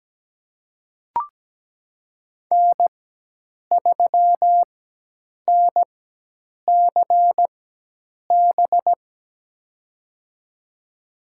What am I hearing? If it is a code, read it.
Morse code, heard as a keyed tone, about 17 wpm, spaced as for 8 wpm, 700 Hz. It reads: N3NCB
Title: USA Call Signs - 5 Characters - 4x Character Spacing - 17wpm